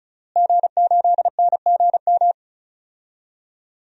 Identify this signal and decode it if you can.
Morse code, heard as a keyed tone, about 36 wpm, 700 Hz. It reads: G8NGM
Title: DXCC-100 Call Signs - 5 Characters - Repeated 3x - 35wpm